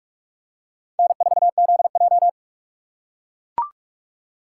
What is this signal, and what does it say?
Morse code, heard as a keyed tone, about 43 wpm, 700 Hz. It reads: N4ZJ
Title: USA Call Signs - 2x Character Spacing - 45wpm